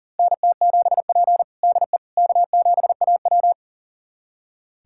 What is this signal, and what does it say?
Morse code, heard as a keyed tone, about 37 wpm, 700 Hz. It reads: NT7P DE K7AW
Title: USA Callsign DE Callsign - 4 Characters - 40wpm